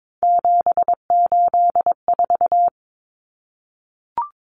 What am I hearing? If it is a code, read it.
Morse code, heard as a keyed tone, about 22 wpm, 700 Hz. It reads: 784
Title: Common Cut Numbers - 3-Digit 22wpm